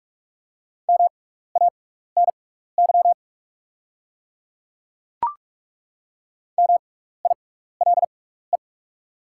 Morse code, heard as a keyed tone, about 43 wpm, spaced as for 15 wpm, 690 Hz. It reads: MANY MILE
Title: 6x Character Spacing - Top 300 Words - 45wpm